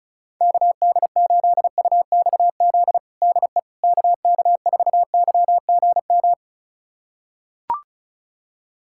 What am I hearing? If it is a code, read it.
Morse code, heard as a keyed tone, about 34 wpm, 700 Hz. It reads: KD8UXZ DE KK4YGM
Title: USA Callsign DE Callsign - 6 Characters - 35wpm